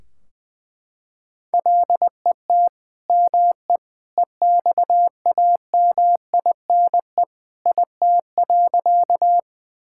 Morse code, heard as a keyed tone, about 20 wpm, 700 Hz. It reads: LET ME EXAMINE IT.
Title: Sentences from Top 1600 Words 20wpm